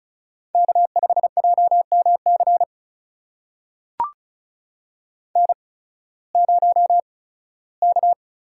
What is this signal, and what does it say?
Morse code, heard as a keyed tone, about 36 wpm, spaced as for 19 wpm, 700 Hz. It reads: K5JMC N0K
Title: USA Call Signs - 8x Character Spacing - 35wpm